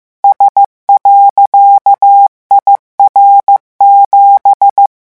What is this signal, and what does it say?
Morse code, heard as a keyed tone, about 15 wpm, 800 Hz. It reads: S.IR7